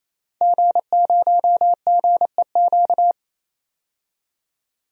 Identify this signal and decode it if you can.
Morse code, heard as a keyed tone, about 28 wpm, 700 Hz. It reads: G0GEQ
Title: DXCC-100 Call Signs - 5 Characters - 28wpm